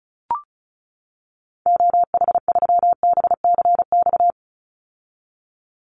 Morse code, heard as a keyed tone, about 36 wpm, 700 Hz. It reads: OH3BCX
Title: Call Signs - 35wpm